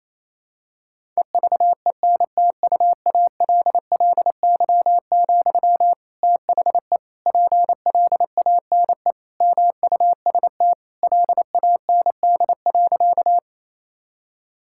Morse code, heard as a keyed tone, about 27 wpm, 700 Hz. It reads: EVENTUALLY, THE PLANE MUST LAND.
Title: Sentences from Top 1000 Words - Review Part 6 - 28wpm